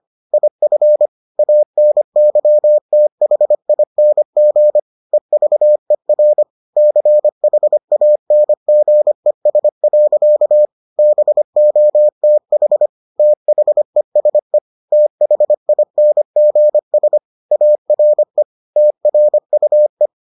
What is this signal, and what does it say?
Morse code, heard as a keyed tone, about 25 wpm, 600 Hz. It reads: IF ANYTHING EVER CHANGES. BOTH THESE THINGS ARE TRUE